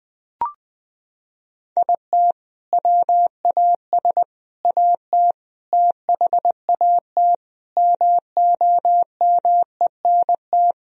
Morse code, heard as a keyed tone, about 21 wpm, 700 Hz. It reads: IT WAS AT THAT MOMENT
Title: Sentences from Top 1000 Words - Review Part 3 - 20wpm